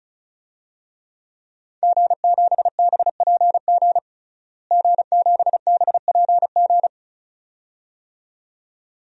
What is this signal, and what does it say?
Morse code, heard as a keyed tone, about 36 wpm, 700 Hz. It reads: G7BPG G7BPG
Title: DXCC-100 Call Signs - 5 Characters - Repeated 2x - 35wpm